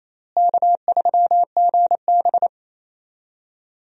Morse code, heard as a keyed tone, about 28 wpm, 700 Hz. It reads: K3GB
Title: USA Call Signs - 4 Characters - Repeated 3x - 28wpm